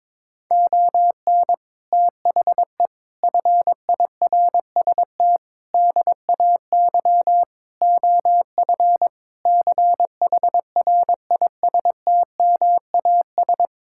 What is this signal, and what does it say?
Morse code, heard as a keyed tone, about 22 wpm, 700 Hz. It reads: ON THE FIRST DAY OF CHRISTMAS
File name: practice / 22wpm / Sentences with 8-Word Phrases - 22wpm